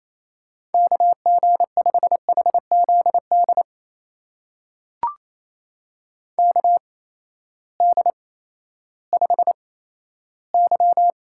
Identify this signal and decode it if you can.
Morse code, heard as a keyed tone, about 28 wpm, spaced as for 14 wpm, 700 Hz. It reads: KG5HZD KD5Y
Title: USA Call Signs - 6 Characters - 8x Character Spacing - 28wpm